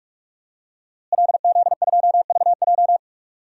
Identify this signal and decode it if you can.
Morse code, heard as a keyed tone, about 47 wpm, 700 Hz. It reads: LZ2VJ
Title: DXCC-100 Call Signs - 5 Characters - Repeated 2x - 45wpm